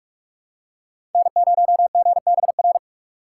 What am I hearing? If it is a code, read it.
Morse code, heard as a keyed tone, about 47 wpm, 700 Hz. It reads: N0GBR
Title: USA Call Signs - 2x Character Spacing - 45wpm